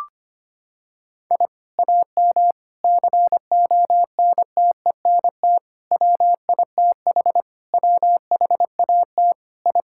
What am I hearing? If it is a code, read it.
Morse code, heard as a keyed tone, about 25 wpm, 700 Hz. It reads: I AM CONTENT WITH WHAT I
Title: Sentences from Top 1000 Words - Review Part 6 - 25wpm